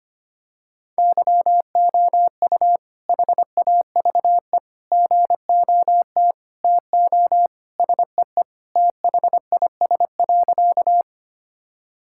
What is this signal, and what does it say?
Morse code, heard as a keyed tone, about 25 wpm, 700 Hz. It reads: YOU HAVE GOT TO SEE THIS.
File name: practice / 25wpm / Sentences with 2-Word Phrases - 25wpm